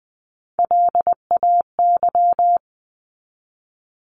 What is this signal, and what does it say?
Morse code, heard as a keyed tone, about 20 wpm, 700 Hz. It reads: LAY
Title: Top 500 Words 20wpm